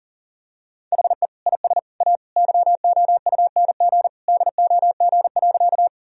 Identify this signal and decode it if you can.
Morse code, heard as a keyed tone, about 37 wpm, 700 Hz. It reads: HE IS A YOUNG DOG.
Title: Sentences from Top 1000 Words - Review Part 2 - 40wpm